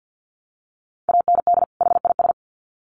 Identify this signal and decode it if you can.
Morse code, heard as a keyed tone, about 50 wpm, 700 Hz. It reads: AND HIS
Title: Sets of 2 Words 50wpm